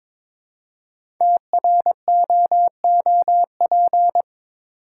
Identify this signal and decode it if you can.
Morse code, heard as a keyed tone, about 23 wpm, 700 Hz. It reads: TROOP
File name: 5 Letter Words - First 3 Letters Given - 22wpm